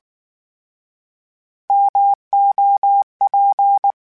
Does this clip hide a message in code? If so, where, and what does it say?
Yes: Morse code from 1.7 to 3.9 s, heard as a keyed tone, about 19 wpm, 800 Hz. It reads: MOP